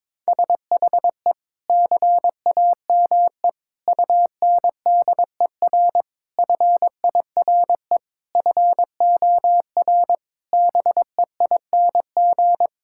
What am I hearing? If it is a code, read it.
Morse code, heard as a keyed tone, about 22 wpm, 700 Hz. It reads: SHE CAME UNDER FIRE FOR BEING
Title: Sentences from Top 1000 Words - Review Part 6 - 22wpm